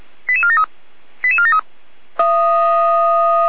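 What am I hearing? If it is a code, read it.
Five-tone selective call (ZVEI), heard as quick runs of five tones, tones of about 70 ms. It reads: 88473 88473
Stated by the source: ZVEI five-tone call received by SDR.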